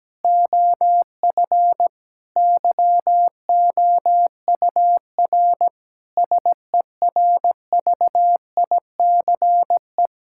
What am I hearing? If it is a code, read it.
Morse code, heard as a keyed tone, about 17 wpm, 700 Hz. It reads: OF YOUR SERVICE
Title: Sentences from Top 1000 Words - Review Part 5 - 17wpm